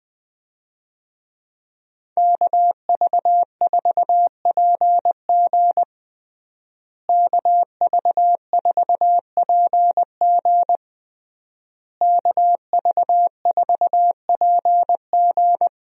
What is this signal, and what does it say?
Morse code, heard as a keyed tone, about 20 wpm, 700 Hz. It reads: KV4PG KV4PG KV4PG
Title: USA Call Signs - 5 Characters - Repeated 3x - 20wpm